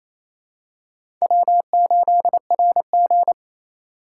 Morse code, heard as a keyed tone, about 28 wpm, 700 Hz. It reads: W8RG
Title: USA Call Signs - 4 Characters - 2x Character Spacing - 28wpm